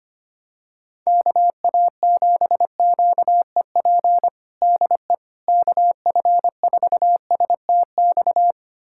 Morse code, heard as a keyed tone, about 25 wpm, 700 Hz. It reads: KA7QEP DE KF4STX
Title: USA Callsign DE Callsign 25wpm